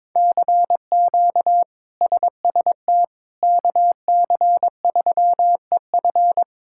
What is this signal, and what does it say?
Morse code, heard as a keyed tone, about 22 wpm, 700 Hz. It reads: CQ SST KC3EF
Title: K1USN SST QSO Practice - 22wpm